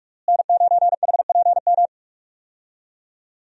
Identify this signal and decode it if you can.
Morse code, heard as a keyed tone, about 46 wpm, 700 Hz. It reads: N9HPM
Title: USA Call Signs - 5 Characters - Repeated 3x - 45wpm